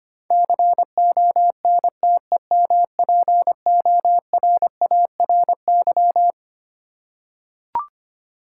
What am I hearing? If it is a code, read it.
Morse code, heard as a keyed tone, about 25 wpm, 700 Hz. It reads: CONTEMPORARY
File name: Twelve Letter Words - 25wpm